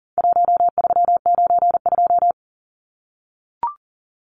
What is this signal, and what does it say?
Morse code, heard as a keyed tone, about 40 wpm, 700 Hz. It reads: 1392
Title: Numbers - Contest Style - 4-Digit 40wpm